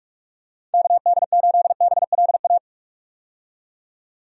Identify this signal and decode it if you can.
Morse code, heard as a keyed tone, about 46 wpm, 700 Hz. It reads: KD8BLA
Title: USA Call Signs - 6 Characters - 45wpm